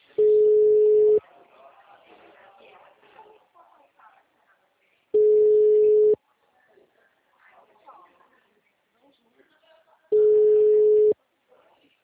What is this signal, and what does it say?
Ringback tone (Europe, 425 Hz), with speech.
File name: out- -2019